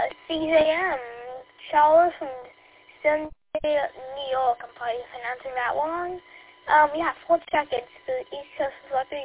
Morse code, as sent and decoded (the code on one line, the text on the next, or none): none